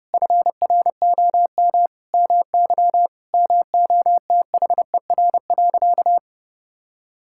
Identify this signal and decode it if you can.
Morse code, heard as a keyed tone, about 32 wpm, 700 Hz. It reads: FROM MY MOTHER.
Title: Sentences from Top 1000 Words - Review Part 6 - 30wpm